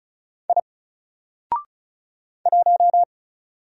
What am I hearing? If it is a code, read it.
Morse code, heard as a keyed tone, about 35 wpm, 700 Hz. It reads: I 1
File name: Single-Letter-Number-35wpm